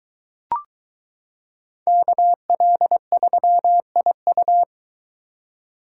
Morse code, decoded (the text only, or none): KL3IU